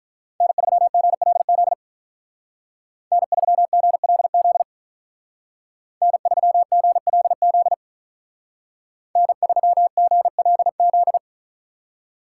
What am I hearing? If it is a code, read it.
Morse code, heard as a keyed tone, about 43 wpm, 700 Hz. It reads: N3GLZ N3GLZ N3GLZ N3GLZ
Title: Speed Racing - USA Call Signs 35wpm